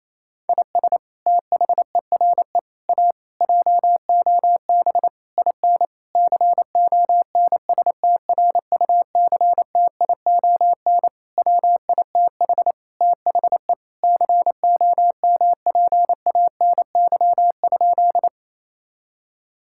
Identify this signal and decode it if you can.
Morse code, heard as a keyed tone, about 28 wpm, 700 Hz. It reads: IS THERE A JOB IN CONSTRUCTION WITH THE COMPANY?
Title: Sentences from Top 1600 Words 28wpm